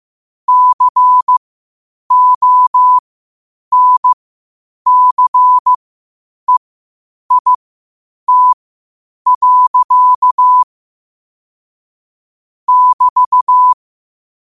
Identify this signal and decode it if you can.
Morse code, heard as a keyed tone, about 15 wpm, spaced as for 8 wpm, 1000 Hz. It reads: CONCEIT. <BT>
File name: qotdfiles / 8wpm